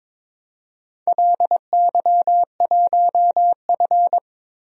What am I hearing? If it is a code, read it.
Morse code, heard as a keyed tone, about 23 wpm, 700 Hz. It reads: LY1F
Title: DXCC-100 Call Signs - 4 Characters - 22wpm